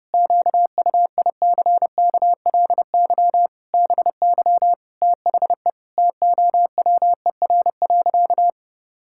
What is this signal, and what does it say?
Morse code, heard as a keyed tone, about 30 wpm, 700 Hz. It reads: QUICKLY BY THE TOWER.
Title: Pangram Challenge v2 - 30wpm